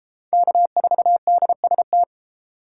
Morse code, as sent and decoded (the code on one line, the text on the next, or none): -.- ....- -.. ... -
K4DST